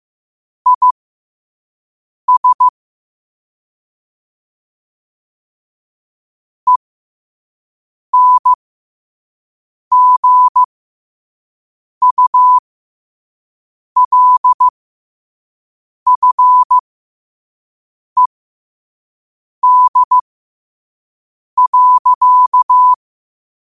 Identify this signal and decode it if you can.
Morse code, heard as a keyed tone, about 15 wpm, spaced as for 5 wpm, 1000 Hz. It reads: IS ENGULFED.
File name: qotdfiles / 5wpm